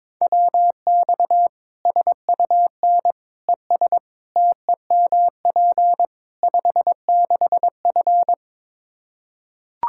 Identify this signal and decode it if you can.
Morse code, heard as a keyed tone, about 22 wpm, 700 Hz. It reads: WX SUN ES TEMP 56F
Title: Practice QSO - Exchange 2 - Info - 22wpm